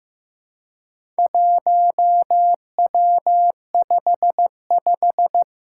Morse code, heard as a keyed tone, about 15 wpm, 700 Hz. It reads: 1W55
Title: Four Challenging Letter-Number Combinations 15wpm